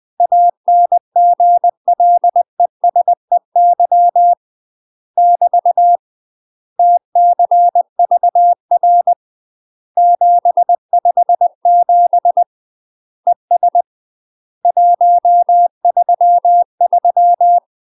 Morse code, heard as a keyed tone, about 20 wpm, 700 Hz. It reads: ANGLESEY <BT> TCVR 757 ES 133
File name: 20WPM